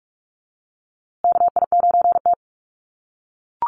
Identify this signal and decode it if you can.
Morse code, heard as a keyed tone, about 44 wpm, 700 Hz. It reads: KI9T
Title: USA Call Signs Repeated 3x 45wpm